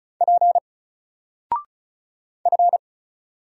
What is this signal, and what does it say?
Morse code, heard as a keyed tone, about 36 wpm, 700 Hz. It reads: P F
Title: Single Letter-Number Alt 35wpm